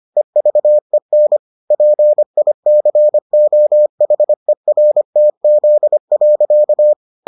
Morse code, heard as a keyed tone, about 25 wpm, 600 Hz. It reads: EVEN PICOHERTZ.